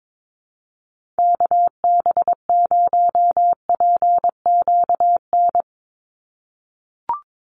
Morse code, heard as a keyed tone, about 22 wpm, 700 Hz. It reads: KB0PQN
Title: USA Call Signs 22wpm